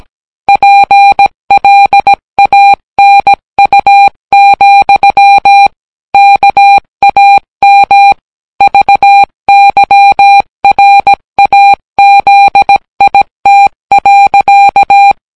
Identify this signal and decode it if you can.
Morse code, heard as a keyed tone, about 18 wpm, 800 Hz. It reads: PLANU, KAM VYRAZIT.